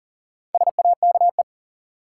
Morse code, heard as a keyed tone, about 39 wpm, 700 Hz. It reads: SAKE
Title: Sets of 4 Words - New Word Drill - 40wpm